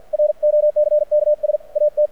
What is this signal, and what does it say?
Morse code, heard as a keyed tone, about 50 wpm, 600 Hz. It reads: ROCKS AT
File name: archived / CRSnet